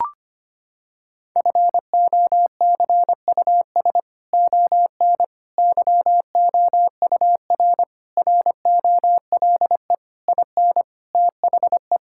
Morse code, heard as a keyed tone, about 25 wpm, 700 Hz. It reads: FOCUS ON YOUR ROLE IN THE
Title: Sentences from Top 1000 Words - Review Part 6 - 25wpm